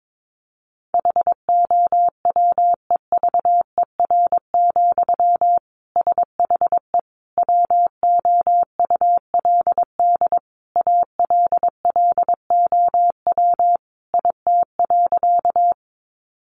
Morse code, heard as a keyed tone, about 22 wpm, 700 Hz. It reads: HOWEVER, SHE WOULD ALLOW IT.
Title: Sentences from Top 700 Words 22wpm